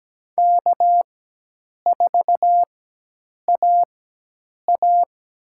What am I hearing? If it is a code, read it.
Morse code, heard as a keyed tone, about 17 wpm, spaced as for 8 wpm, 700 Hz. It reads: K4AA
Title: USA Call Signs - 4x Character Spacing - 17wpm